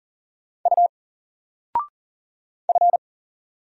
Morse code, heard as a keyed tone, about 39 wpm, 700 Hz. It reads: U F
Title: Single Character (Letter:Number:Punctuation:Prosign) 40wpm